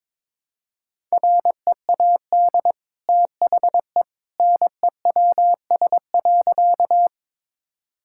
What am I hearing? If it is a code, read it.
Morse code, heard as a keyed tone, about 22 wpm, 690 Hz. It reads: READ THE NEWS.
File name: practice / 22wpm / Easy Sentences with Hints - 22wpm